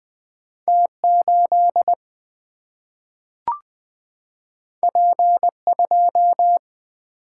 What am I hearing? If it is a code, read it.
Morse code, heard as a keyed tone, about 20 wpm, 700 Hz. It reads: T8 P2
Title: Single-Letter Single-Number 20wpm